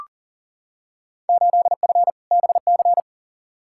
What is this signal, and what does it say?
Morse code, heard as a keyed tone, about 39 wpm, 700 Hz. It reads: 8F BC
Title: Field Day Exchanges Encore - 40wpm